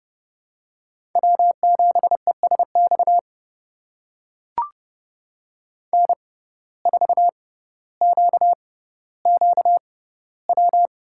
Morse code, heard as a keyed tone, about 31 wpm, spaced as for 16 wpm, 700 Hz. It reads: W7ESX N4QQW
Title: USA Call Signs - 5 Characters - 6x Character Spacing - 30wpm